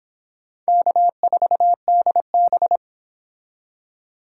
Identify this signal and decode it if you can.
Morse code, heard as a keyed tone, about 26 wpm, 700 Hz. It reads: K4DB